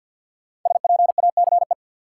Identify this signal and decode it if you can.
Morse code, heard as a keyed tone, about 52 wpm, 700 Hz. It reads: SPACE
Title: Sentences from Top 1500 Words - New Word Drill Alt - 50wpm